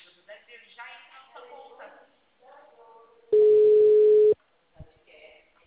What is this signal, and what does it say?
Ringback tone (Europe, 425 Hz), with speech.
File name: out- -2011- - - .24189